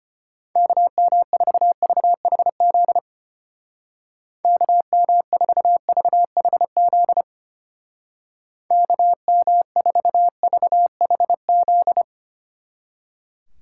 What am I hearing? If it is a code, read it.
Morse code, heard as a keyed tone, about 32 wpm, 700 Hz. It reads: KM4VHZ KM4VHZ KM4VHZ